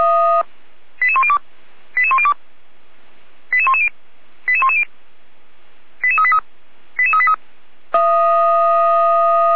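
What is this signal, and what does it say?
Five-tone selective call (ZVEI), heard as quick runs of five tones, tones of about 70 ms. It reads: 88102 88102 88110 88110 88383 88383
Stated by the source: ZVEI five-tone call received by SDR, logged by Feuerwehr Orlamünde.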